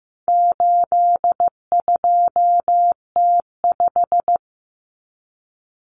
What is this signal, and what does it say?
Morse code, heard as a keyed tone, about 15 wpm, 700 Hz. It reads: 82T5
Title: Common Cut Numbers - 4-Digit 15wpm